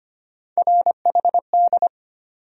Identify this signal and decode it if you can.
Morse code, heard as a keyed tone, about 25 wpm, 700 Hz. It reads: RHD
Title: USA Call Sign Postfixes - 3 Letters 25wpm